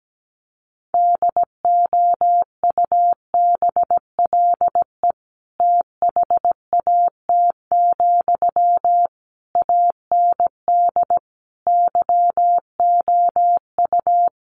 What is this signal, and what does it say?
Morse code, heard as a keyed tone, about 17 wpm, 700 Hz. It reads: DOUBLE THAT, AND YOU
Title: Sentences from Top 1400 Words 17wpm